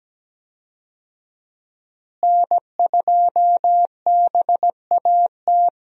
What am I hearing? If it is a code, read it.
Morse code, heard as a keyed tone, about 17 wpm, 700 Hz. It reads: N2BAT
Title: USA Call Signs - 5 Characters - Repeated 2x - 17wpm